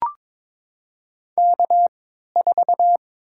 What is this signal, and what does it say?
Morse code, heard as a keyed tone, about 22 wpm, spaced as for 13 wpm, 700 Hz. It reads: K4
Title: USA Call Signs - 5 Characters - 3x Character Spacing - 22wpm